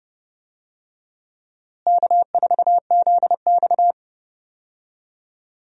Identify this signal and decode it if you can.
Morse code, heard as a keyed tone, about 30 wpm, 700 Hz. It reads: K4ZX
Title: USA Call Signs - 4 Characters - Repeated 3x - 30wpm